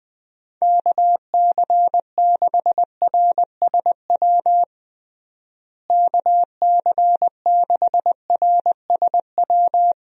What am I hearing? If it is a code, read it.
Morse code, heard as a keyed tone, about 20 wpm, 700 Hz. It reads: KC6RSW KC6RSW
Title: USA Call Signs - 6 Characters - Repeated 3x - 20wpm